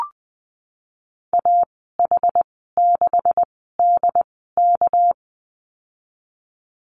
Morse code, heard as a keyed tone, about 20 wpm, spaced as for 15 wpm, 700 Hz. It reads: AH6DK
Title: 238 Lesson 38g - Letter q - Call Signs 20wpm